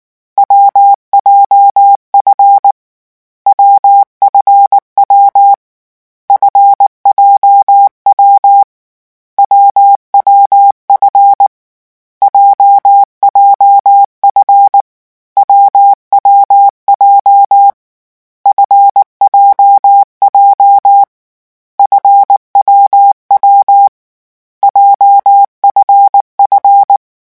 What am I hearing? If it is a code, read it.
Morse code, heard as a keyed tone, about 19 wpm, 800 Hz. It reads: WJF WFW FJW WWF JJF WWJ FJJ FWW JFF
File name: morse mp3 / lecon7-vitesse 18